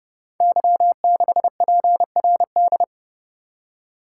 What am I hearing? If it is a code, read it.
Morse code, heard as a keyed tone, about 30 wpm, 700 Hz. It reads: Y6PRD
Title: Code Groups - 5 Letter:Numbers - 30wpm